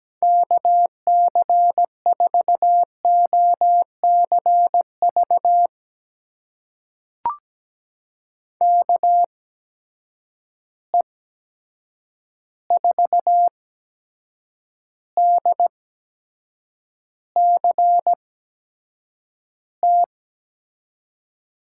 Morse code, heard as a keyed tone, about 17 wpm, spaced as for 7 wpm, 700 Hz. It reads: KC4OCV KE4DCT
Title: USA Call Signs - 6 Characters - 8x Character Spacing - 17wpm